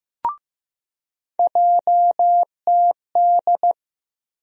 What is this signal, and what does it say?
Morse code, heard as a keyed tone, about 16 wpm, 700 Hz. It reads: JTD
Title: USA Call Sign Postfixes - 3 Letters 15wpm